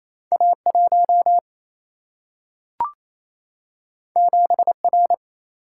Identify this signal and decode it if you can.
Morse code, heard as a keyed tone, about 28 wpm, 700 Hz. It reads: A 1 7 R
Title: Single-Letter Single-Number 28wpm